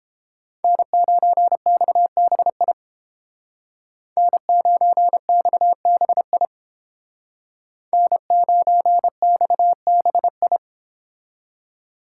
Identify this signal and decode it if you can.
Morse code, heard as a keyed tone, about 30 wpm, 700 Hz. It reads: N9XBI N9XBI N9XBI